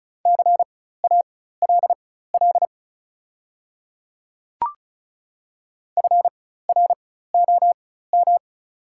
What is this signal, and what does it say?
Morse code, heard as a keyed tone, about 36 wpm, spaced as for 16 wpm, 690 Hz. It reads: CALL FROM
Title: 4x Character Spacing - Top 100 Words - 35wpm